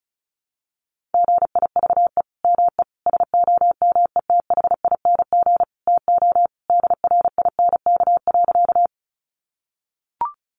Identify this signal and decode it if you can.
Morse code, heard as a keyed tone, about 34 wpm, 700 Hz. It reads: GIVE ME SOMETHING TO DRINK.
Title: Sentences from Top 1500 Words 35wpm